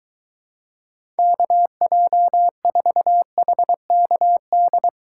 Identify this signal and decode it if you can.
Morse code, heard as a keyed tone, about 23 wpm, 700 Hz. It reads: KJ4HKD